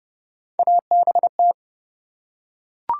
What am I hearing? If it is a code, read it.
Morse code, heard as a keyed tone, about 30 wpm, 700 Hz. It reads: ABT
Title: Top 90 QSO Elements 30wpm